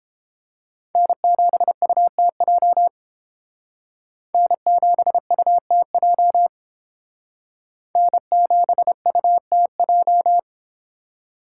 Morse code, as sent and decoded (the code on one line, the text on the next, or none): -. --... ..- - .--- / -. --... ..- - .--- / -. --... ..- - .---
N7UTJ N7UTJ N7UTJ